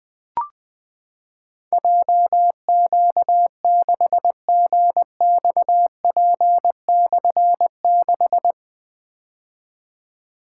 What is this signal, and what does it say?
Morse code, heard as a keyed tone, about 20 wpm, 700 Hz. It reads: JQ6GXP/6
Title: Difficult Call Signs 20wpm